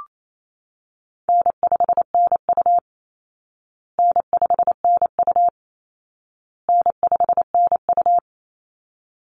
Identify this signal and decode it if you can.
Morse code, heard as a keyed tone, about 28 wpm, 700 Hz. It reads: N5NU N5NU N5NU
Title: Easy Call Signs Repeated 3x - 28wpm